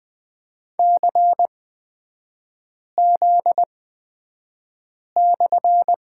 Morse code, heard as a keyed tone, about 20 wpm, spaced as for 5 wpm, 700 Hz. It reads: CZ/
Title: 243 Lesson 39e - Letter z - Three Characters 20wpm